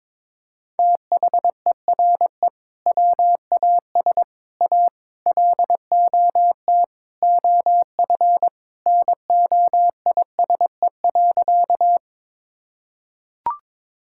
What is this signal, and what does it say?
Morse code, heard as a keyed tone, about 22 wpm, 700 Hz. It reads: THERE WAS A LOT OF NOISE.
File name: practice / 22wpm / Sentences with 4-Word Phrases - 22wpm